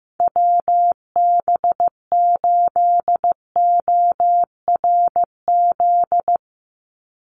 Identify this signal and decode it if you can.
Morse code, heard as a keyed tone, about 15 wpm, 700 Hz. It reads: WB8ORZ
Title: USA Call Signs 15wpm